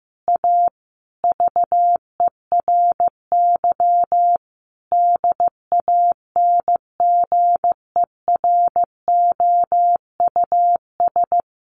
Morse code, heard as a keyed tone, about 15 wpm, 700 Hz. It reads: A VERY DANGEROUS
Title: Sentences from Top 1200 Words 15wpm